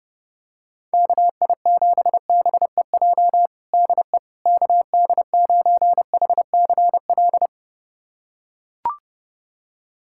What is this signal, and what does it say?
Morse code, heard as a keyed tone, about 32 wpm, 700 Hz. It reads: KI7BEJ DE KD9HCL
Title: USA Callsign DE Callsign - 6 Characters - 30wpm